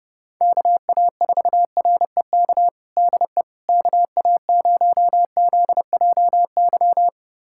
Morse code, heard as a keyed tone, about 30 wpm, 700 Hz. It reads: KA4REK DE KA0ZJY
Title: USA Callsign DE Callsign 30wpm